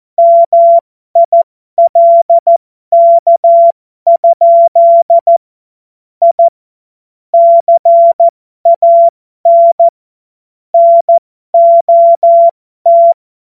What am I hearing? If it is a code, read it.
Morse code, heard as a keyed tone, about 14 wpm, spaced as for 12 wpm, 680 Hz. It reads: MILK? I CAN NOT
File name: Morsekurs / Witze